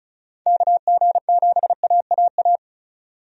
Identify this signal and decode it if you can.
Morse code, heard as a keyed tone, about 34 wpm, 700 Hz. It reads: KG7AAA
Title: USA Call Signs - 6 Characters - 35wpm